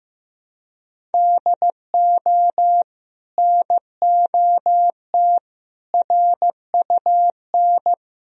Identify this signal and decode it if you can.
Morse code, heard as a keyed tone, about 15 wpm, 700 Hz. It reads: DO NOT RUN
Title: Sentences from Top 1000 Words - Review Part 1 - 15wpm